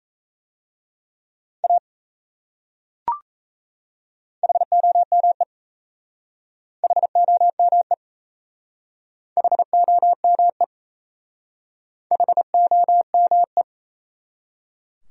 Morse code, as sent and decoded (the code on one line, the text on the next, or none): .- / .... --- -- . / .... --- -- . / .... --- -- . / .... --- -- .
A HOME HOME HOME HOME